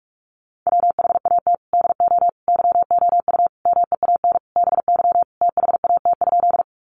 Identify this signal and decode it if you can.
Morse code, heard as a keyed tone, about 44 wpm, 700 Hz. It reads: WHAT DO YOU MEAN BY THAT?
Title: Sentences from Top 200 Words 45wpm